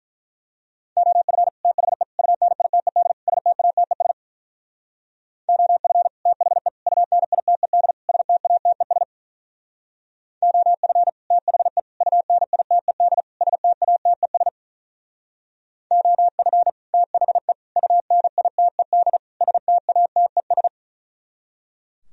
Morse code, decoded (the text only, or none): OF THE UNITED STATES OF THE UNITED STATES OF THE UNITED STATES OF THE UNITED STATES